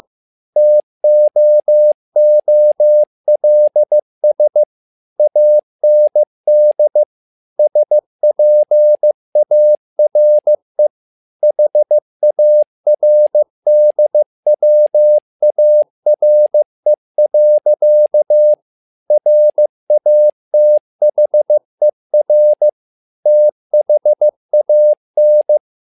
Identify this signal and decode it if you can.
Morse code, heard as a keyed tone, about 15 wpm, 600 Hz. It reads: TOOLS AND SPARE HARDWARE. RATHER THAN
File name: learning center / tower trouble / tower trouble 15wpm